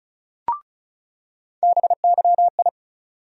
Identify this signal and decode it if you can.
Morse code, heard as a keyed tone, about 36 wpm, 700 Hz. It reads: BYI
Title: USA Call Sign Postfixes - 3 Letters 35wpm